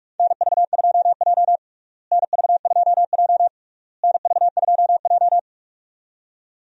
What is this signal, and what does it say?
Morse code, heard as a keyed tone, about 46 wpm, 700 Hz. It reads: NV2J NV2J NV2J
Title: USA Call Signs - 4 Characters - Repeated 3x - 45wpm